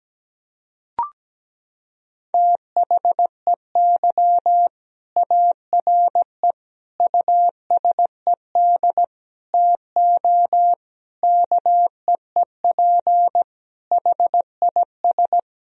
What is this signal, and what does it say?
Morse code, heard as a keyed tone, about 17 wpm, 700 Hz. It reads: THEY ARE USED TO KEEP HIS
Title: Sentences from Top 1000 Words - Review Part 3 - 17wpm